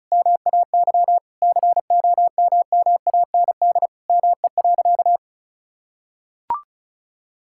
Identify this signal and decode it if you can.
Morse code, heard as a keyed tone, about 34 wpm, 700 Hz. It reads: MAY COMMAND ME.
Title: Sentences from Top 1600 Words 35wpm